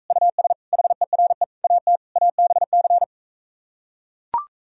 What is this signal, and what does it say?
Morse code, heard as a keyed tone, about 44 wpm, 700 Hz. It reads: US HERE AT ABC